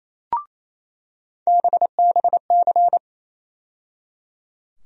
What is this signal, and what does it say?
Morse code, heard as a keyed tone, about 28 wpm, 700 Hz. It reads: BBC
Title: USA Call Sign Postfixes - 3 Letters 28wpm